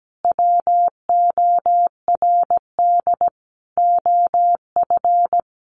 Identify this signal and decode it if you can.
Morse code, heard as a keyed tone, about 17 wpm, 700 Hz. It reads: WORD OF
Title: Sentences from Top 1100 Words 17wpm